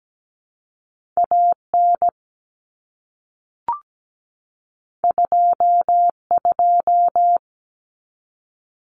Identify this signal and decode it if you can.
Morse code, heard as a keyed tone, about 17 wpm, 700 Hz. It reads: AN 22
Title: Common Cut Numbers - 2-Digit 17wpm